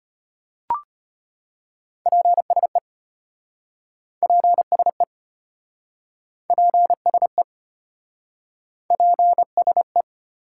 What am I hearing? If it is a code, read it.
Morse code, heard as a keyed tone, about 34 wpm, 700 Hz. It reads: PSE PSE PSE PSE